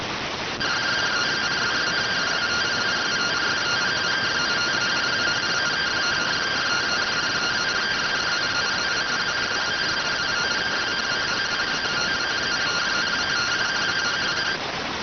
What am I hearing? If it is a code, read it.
Two-tone teleprinter signal, about 170 Hz shift.